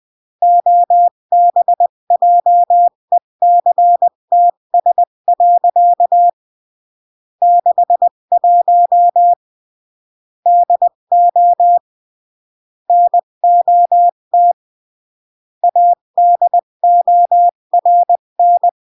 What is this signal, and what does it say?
Morse code, heard as a keyed tone, about 20 wpm, 700 Hz. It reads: OBJECTS. 61 DO NOT ADORN
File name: CW / jodorowsky / 82 commandments 20-15